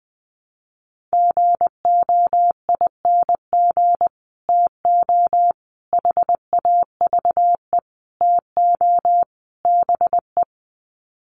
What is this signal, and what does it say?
Morse code, heard as a keyed tone, about 20 wpm, 700 Hz. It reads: GOING TO HAVE TO BE
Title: Sets of 5 Words 20wpm